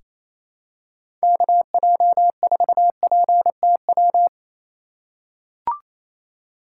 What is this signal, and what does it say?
Morse code, heard as a keyed tone, about 28 wpm, 700 Hz. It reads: KJ4PTW
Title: USA Call Signs - 6 Characters - 4x Character Spacing - 28wpm